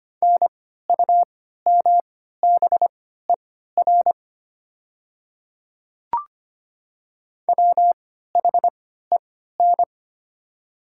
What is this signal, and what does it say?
Morse code, heard as a keyed tone, about 25 wpm, spaced as for 14 wpm, 700 Hz. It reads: NUMBER WHEN
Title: 3x Character Spacing - Top 100 Words - 25wpm